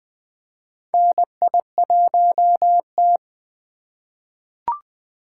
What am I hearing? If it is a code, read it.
Morse code, heard as a keyed tone, about 20 wpm, 700 Hz. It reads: NI1T
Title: USA Call Signs - 4 Characters - 3x Character Spacing - 20wpm